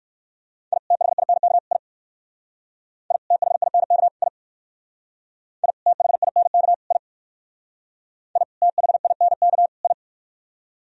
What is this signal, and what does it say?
Morse code, heard as a keyed tone, about 57 wpm, 700 Hz. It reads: I THINK I I THINK I I THINK I I THINK I